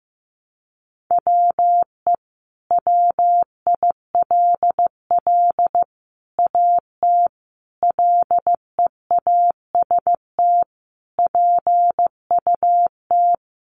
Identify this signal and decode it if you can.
Morse code, heard as a keyed tone, about 15 wpm, 700 Hz. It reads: WE WILL AT LEAST PUT
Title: Sentences from Top 1200 Words 15wpm